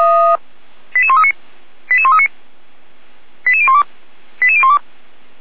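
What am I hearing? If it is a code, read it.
Five-tone selective call (ZVEI), heard as quick runs of five tones, tones of about 70 ms. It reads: 88139 88139 88012 88012
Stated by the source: ZVEI five-tone selective call received on SDR.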